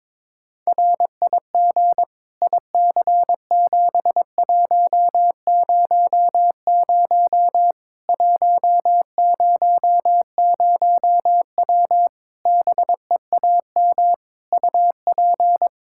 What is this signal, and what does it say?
Morse code, heard as a keyed tone, about 22 wpm, 700 Hz. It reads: RIG IC7100 100W BEAM UP
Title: Practice QSO - Exchange 2 - Info - 22wpm